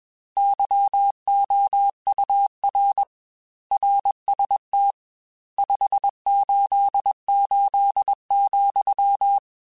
Morse code, heard as a keyed tone, about 21 wpm, 800 Hz. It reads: YOUR RST 588,
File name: RSTs / RST-mp3s / Your RSTs @20